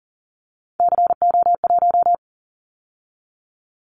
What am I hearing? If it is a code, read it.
Morse code, heard as a keyed tone, about 40 wpm, 700 Hz. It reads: CO1
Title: Callsign Prefixes for Top 100 DXCC - 40wpm